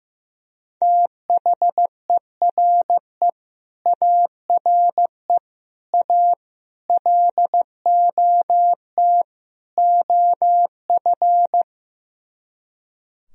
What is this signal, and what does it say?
Morse code, heard as a keyed tone, about 15 wpm, 700 Hz. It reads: THERE ARE A LOT OF
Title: Speed-Racing - Sets of 5 Words 15wpm